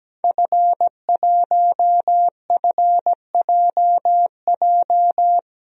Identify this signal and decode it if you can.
Morse code, heard as a keyed tone, about 17 wpm, 700 Hz. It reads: F1FJJ
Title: DXCC-100 Call Signs - 5 Characters - 17wpm